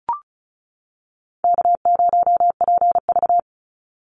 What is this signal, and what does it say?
Morse code, heard as a keyed tone, about 36 wpm, 700 Hz. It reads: K0PV
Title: Call Signs - 35wpm